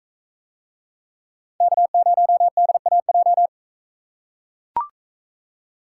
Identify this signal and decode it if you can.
Morse code, heard as a keyed tone, about 41 wpm, 700 Hz. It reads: K0DAJ